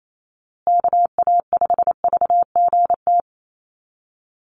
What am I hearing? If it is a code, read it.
Morse code, heard as a keyed tone, about 28 wpm, 700 Hz. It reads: KA5VGT
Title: USA Call Signs Repeated 3x 28wpm